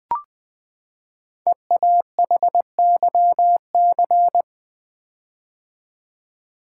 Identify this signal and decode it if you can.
Morse code, heard as a keyed tone, about 20 wpm, 700 Hz. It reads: EAHYC
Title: Code Groups - 5 Letters - 20wpm